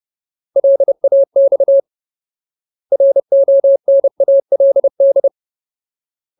Morse code, heard as a keyed tone, about 30 wpm, 550 Hz. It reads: LAX RONALD
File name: Ultimate Word List / 30WPM Ultimate